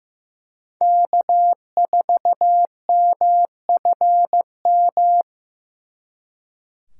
Morse code, heard as a keyed tone, about 18 wpm, 700 Hz. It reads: K4MFM
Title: Speed Racing - USA 1x3 Call Signs 15wpm